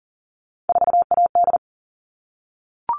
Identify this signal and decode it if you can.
Morse code, heard as a keyed tone, about 40 wpm, 700 Hz. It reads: 4AD
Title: All Cut Numbers - 3-Digit 40wpm